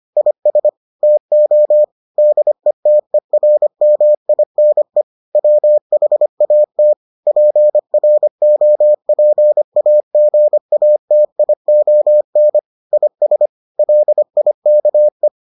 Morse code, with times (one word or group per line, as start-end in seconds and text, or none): 0.2-0.7 IS
1.0-1.8 TO
2.2-5.0 DETERMINE
5.4-6.9 WHAT
7.3-12.6 PROPAGATION
12.9-13.5 IS
13.8-15.3 LIKE